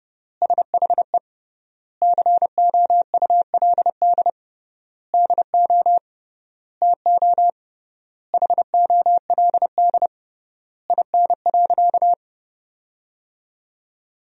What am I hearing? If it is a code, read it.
Morse code, heard as a keyed tone, about 30 wpm, 700 Hz. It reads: SHE COULD DO TO HOLD IN.
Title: 3x Word-Spacing - Sentences from Top 400 Words - 30wpm